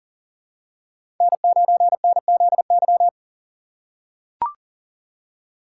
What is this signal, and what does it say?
Morse code, heard as a keyed tone, about 40 wpm, 700 Hz. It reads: N9NZY
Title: USA Call Signs - 5 Characters - 6x Character Spacing - 40wpm